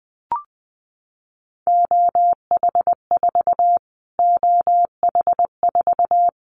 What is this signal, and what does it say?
Morse code, heard as a keyed tone, about 20 wpm, 700 Hz. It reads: OH4 OH4
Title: Callsign Prefixes for Top 25 DXCC - Repeated 3x - 20wpm